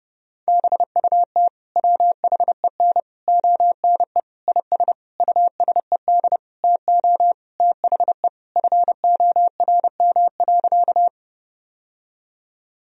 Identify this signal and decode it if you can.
Morse code, heard as a keyed tone, about 32 wpm, 700 Hz. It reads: BUT WHEN ONE IS USED TO THE FORM.
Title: Sentences from Top 1000 Words - Review Part 1 - 30wpm